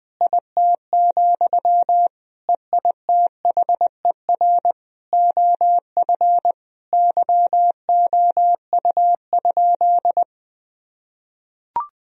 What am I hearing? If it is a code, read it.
Morse code, heard as a keyed tone, about 21 wpm, 700 Hz. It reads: IT, EITHER OF YOU?
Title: Sentences from Top 1000 Words - Review Part 4 - 20wpm